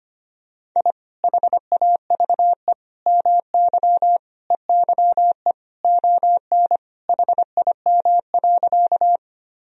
Morse code, heard as a keyed tone, about 25 wpm, 700 Hz. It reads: I HAVE MY EYE ON HIM.
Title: Sentences from Top 1000 Words - Review Part 1 - 25wpm